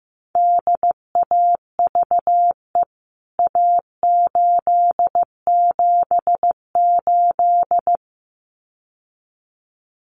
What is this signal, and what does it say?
Morse code, heard as a keyed tone, about 15 wpm, 700 Hz. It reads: DAVE A878
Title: CWT Exchange 15wpm